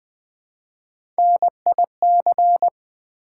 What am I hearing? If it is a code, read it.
Morse code, heard as a keyed tone, about 20 wpm, 700 Hz. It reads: NIC
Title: Build A Word - 1 Letter At A Time - 5-Letter Words - 20wpm